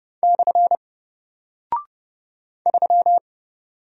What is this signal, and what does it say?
Morse code, heard as a keyed tone, about 30 wpm, 700 Hz. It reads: / 3
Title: Single Character (Letter:Number:Punctuation:Prosign) Alt 30wpm